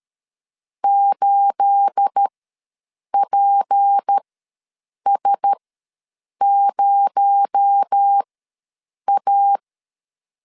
Morse code, heard as a keyed tone, about 13 wpm, spaced as for 7 wpm, 800 Hz. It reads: 8PS0A
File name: LEZIONI / 9 28 ws 12wpm 3354